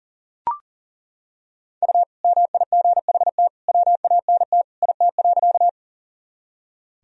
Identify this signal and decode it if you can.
Morse code, heard as a keyed tone, about 37 wpm, 700 Hz. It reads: U MIGHT WANT IT.
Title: Sentences from Top 300 Words and Abbreviations 40wpm